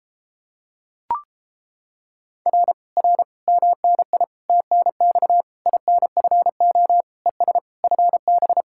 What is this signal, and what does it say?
Morse code, heard as a keyed tone, about 31 wpm, 700 Hz. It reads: R R MNI TNX INFO ES FB